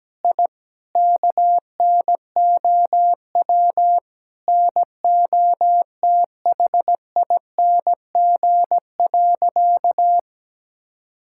Morse code, heard as a keyed tone, about 17 wpm, 700 Hz. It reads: I KNOW NOTHING.
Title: Sentences from Top 1000 Words - Review Part 6 - 17wpm